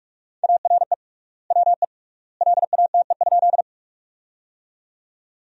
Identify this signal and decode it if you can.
Morse code, heard as a keyed tone, about 43 wpm, 700 Hz. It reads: ARE WE LATE?
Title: 3x Word-Spacing - Sentences from Top 300 Words - 45wpm